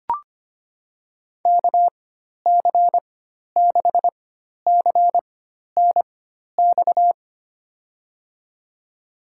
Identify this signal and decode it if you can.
Morse code, heard as a keyed tone, about 25 wpm, spaced as for 12 wpm, 700 Hz. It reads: KC6CNX
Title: USA Call Signs - 6 Characters - 4x Character Spacing - 25wpm